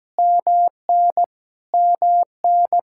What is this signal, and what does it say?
Morse code, heard as a keyed tone, about 17 wpm, 700 Hz. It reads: MN MN
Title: Practice QSO - Exchange 1 - Beginning - 17wpm